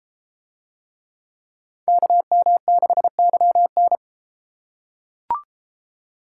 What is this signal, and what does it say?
Morse code, heard as a keyed tone, about 30 wpm, 700 Hz. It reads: KM6YN